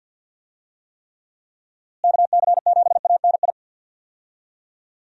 Morse code, heard as a keyed tone, about 51 wpm, 700 Hz. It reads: KC7ANI
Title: USA Call Signs - 6 Characters - Repeated 3x - 50wpm